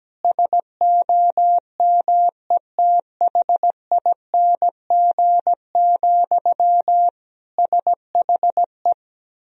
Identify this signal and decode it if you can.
Morse code, heard as a keyed tone, about 17 wpm, 700 Hz. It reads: SOMETHING, SHE
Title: Sentences from Top 1000 Words - Review Part 3 - 17wpm